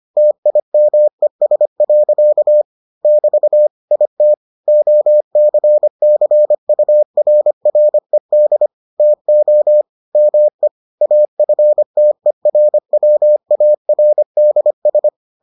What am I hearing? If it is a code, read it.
Morse code, heard as a keyed tone, about 25 wpm, 600 Hz. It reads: TIMES. <BT> IT OCCURRED TO ME AFTERWARDS